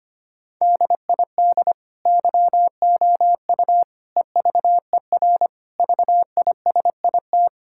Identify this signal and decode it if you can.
Morse code, heard as a keyed tone, about 25 wpm, 700 Hz. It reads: DID YOU EVER VISIT
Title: Sentences from Top 1000 Words - Review Part 6 - 25wpm